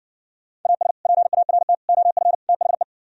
Morse code, heard as a keyed tone, about 59 wpm, 700 Hz. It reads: AS PART OF THE